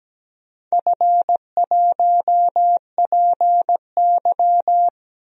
Morse code, heard as a keyed tone, about 17 wpm, 700 Hz. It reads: F1PY
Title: DXCC-100 Call Signs - 4 Characters - Repeated 3x - 17wpm